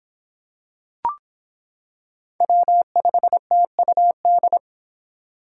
Morse code, heard as a keyed tone, about 25 wpm, 700 Hz. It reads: W5TUD